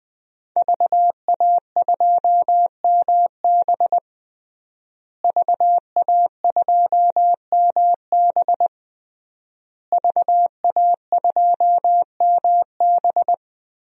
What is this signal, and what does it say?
Morse code, heard as a keyed tone, about 20 wpm, 700 Hz. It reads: VA2MB VA2MB VA2MB
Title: DXCC-100 Call Signs - 5 Characters - Repeated 3x - 20wpm